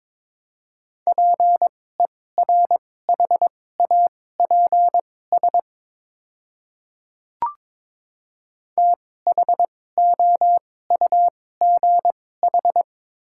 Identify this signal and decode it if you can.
Morse code, heard as a keyed tone, about 22 wpm, spaced as for 16 wpm, 700 Hz. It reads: PERHAPS THOUGH
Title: 2x Character Spacing - Top 500 Words - 22wpm